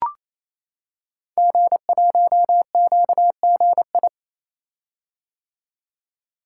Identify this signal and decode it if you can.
Morse code, heard as a keyed tone, about 27 wpm, 700 Hz. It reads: G1QGI
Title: Code Groups - 5 Letter:Numbers - 28wpm